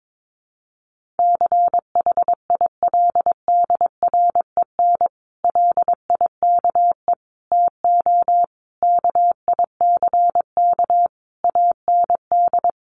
Morse code, heard as a keyed tone, about 22 wpm, 700 Hz. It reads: CHILDREN LIKE TO KICK AND
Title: Sentences from Top 1300 Words 22wpm